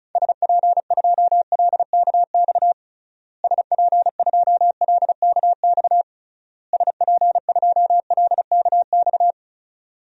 Morse code, heard as a keyed tone, about 36 wpm, 700 Hz. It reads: SP2LKX SP2LKX SP2LKX
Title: DXCC-100 Call Signs - 6 Characters - Repeated 3x - 35wpm